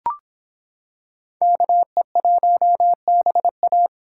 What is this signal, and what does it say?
Morse code, heard as a keyed tone, about 26 wpm, 700 Hz. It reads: KE1BA